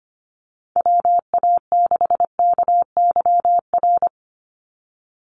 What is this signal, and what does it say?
Morse code, heard as a keyed tone, about 25 wpm, 700 Hz. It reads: WA6KYR
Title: NA SOTA Chaser Callsigns 25wpm